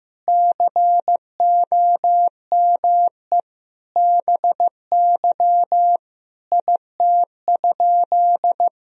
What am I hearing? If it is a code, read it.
Morse code, heard as a keyed tone, about 15 wpm, 700 Hz. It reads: COME BY IT?
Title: Sentences from Top 100 Words and Abbreviations 15wpm